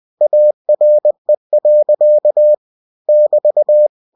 Morse code, heard as a keyed tone, about 20 wpm, 600 Hz. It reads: ARE. <BT>
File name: cw-books / winnie-the-pooh / winnie-20wpm-page-0028